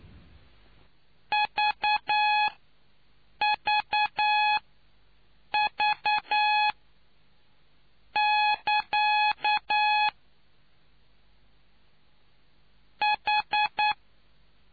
Morse code, heard as a keyed tone, about 9 wpm, spaced as for 5 wpm, 840 Hz. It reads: VVV<KA> H